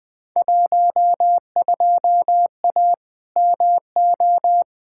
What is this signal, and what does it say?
Morse code, heard as a keyed tone, about 20 wpm, 700 Hz. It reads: 12A MO
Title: Field Day Exchanges Encore - 20wpm